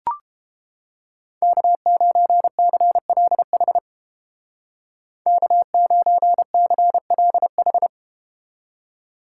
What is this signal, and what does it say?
Morse code, heard as a keyed tone, about 30 wpm, 700 Hz. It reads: K9CLH K9CLH